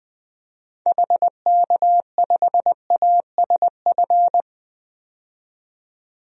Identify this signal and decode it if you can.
Morse code, heard as a keyed tone, about 20 wpm, 700 Hz. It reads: HK5ASF
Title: DXCC-100 Call Signs - 6 Characters - 20wpm